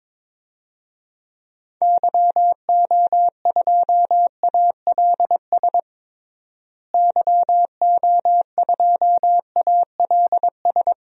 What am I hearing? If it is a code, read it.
Morse code, heard as a keyed tone, about 22 wpm, 700 Hz. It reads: YO2ALS YO2ALS
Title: DXCC-100 Call Signs - 6 Characters - Repeated 3x - 22wpm